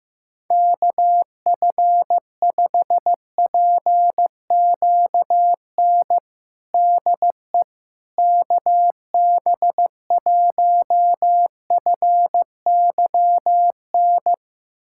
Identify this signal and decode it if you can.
Morse code, heard as a keyed tone, about 15 wpm, 700 Hz. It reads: KF5PQN DE KB1FYN
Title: USA Callsign DE Callsign - 6 Characters - 15wpm